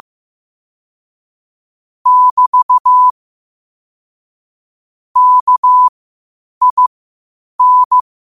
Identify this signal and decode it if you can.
Morse code, heard as a keyed tone, about 15 wpm, spaced as for 8 wpm, 1000 Hz. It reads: <BT> KIN
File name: qotdfiles / 8wpm